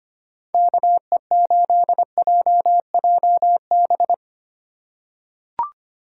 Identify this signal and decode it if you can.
Morse code, heard as a keyed tone, about 25 wpm, 700 Hz. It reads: KE8JJB
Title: USA Call Signs - 6 Characters - 3x Character Spacing - 25wpm